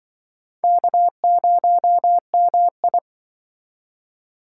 Morse code, heard as a keyed tone, about 24 wpm, 700 Hz. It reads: K0MI
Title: Speed Racing - USA Call Signs 20wpm